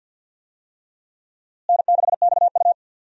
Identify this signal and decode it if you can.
Morse code, heard as a keyed tone, about 49 wpm, 700 Hz. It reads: N6XU
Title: USA Call Signs - 4 Characters - Repeated 3x - 50wpm